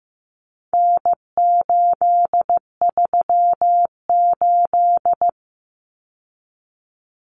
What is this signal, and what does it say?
Morse code, heard as a keyed tone, about 15 wpm, 700 Hz. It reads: N838
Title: Common Cut Numbers - 4-Digit 15wpm